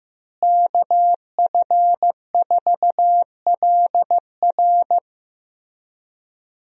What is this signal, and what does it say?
Morse code, heard as a keyed tone, about 16 wpm, 700 Hz. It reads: KF4LR
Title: USA Call Signs - 5 Characters - Repeated 3x - 15wpm